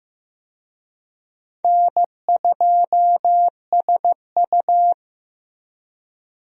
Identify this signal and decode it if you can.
Morse code, heard as a keyed tone, about 15 wpm, 700 Hz. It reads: N2SU
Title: USA Call Signs - 4 Characters - Repeated 3x - 15wpm